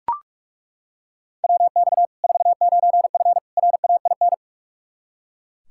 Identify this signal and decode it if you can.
Morse code, heard as a keyed tone, about 42 wpm, 700 Hz. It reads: WX 49F RAIN
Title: Practice QSO - Exchange 2 - Info - 45wpm